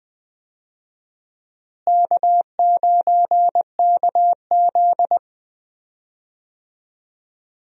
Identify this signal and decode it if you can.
Morse code, heard as a keyed tone, about 21 wpm, 700 Hz. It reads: K9KZ
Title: USA Call Signs - 4 Characters - Repeated 3x - 20wpm